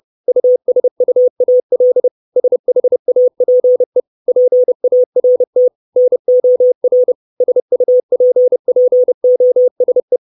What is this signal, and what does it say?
Morse code, heard as a keyed tone, about 30 wpm, 500 Hz. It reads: USUAL SHAPE PART NOR SUPPOSE
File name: 632 Words 30WPM